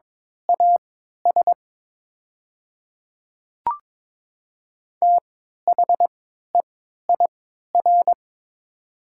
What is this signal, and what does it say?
Morse code, heard as a keyed tone, about 22 wpm, spaced as for 13 wpm, 700 Hz. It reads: AS THEIR
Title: 3x Character Spacing - Top 100 Words - 22wpm